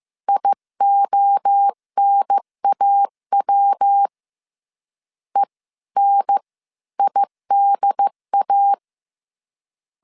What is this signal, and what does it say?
Morse code, heard as a keyed tone, about 15 wpm, spaced as for 13 wpm, 800 Hz. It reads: IONAW ENIDA
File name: CW / DET / 1 3 1500C S C 14wpm 2458